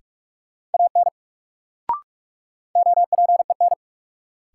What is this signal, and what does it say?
Morse code, heard as a keyed tone, about 46 wpm, 700 Hz. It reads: AN OPEN
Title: Sentences from Top 300 Words - New Word Drill - 45wpm